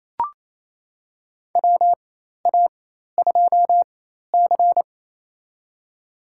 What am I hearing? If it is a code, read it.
Morse code, heard as a keyed tone, about 29 wpm, spaced as for 13 wpm, 700 Hz. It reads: WA2C
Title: USA Call Signs - 4 Characters - 4x Character Spacing - 28wpm